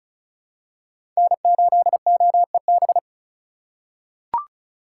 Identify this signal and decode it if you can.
Morse code, heard as a keyed tone, about 36 wpm, 700 Hz. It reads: N8OEB
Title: USA Call Signs - 5 Characters - 4x Character Spacing - 35wpm